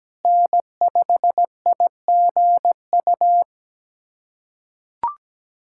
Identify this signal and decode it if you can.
Morse code, heard as a keyed tone, about 17 wpm, 700 Hz. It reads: N5IGU
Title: USA Callsign DE Callsign - 5 Characters - 17wpm